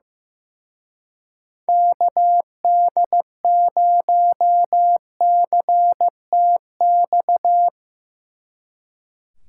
Morse code, heard as a keyed tone, about 17 wpm, 700 Hz. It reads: KD0CTX
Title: Speed Racing - USA 2x3 Call Signs 15wpm